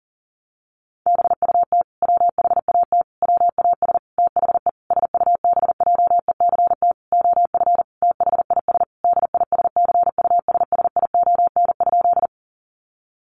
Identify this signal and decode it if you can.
Morse code, heard as a keyed tone, about 40 wpm, 700 Hz. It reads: BUT WHAT WAS THE SUBJECT OF THIS DISCUSSION?
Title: Sentences from Top 1100 Words 40wpm